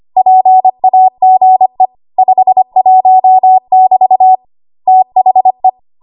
Morse code, heard as a keyed tone, about 25 wpm, 750 Hz. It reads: PAGE 51 <BT> THE
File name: Archive / 25 WPM / 25WPM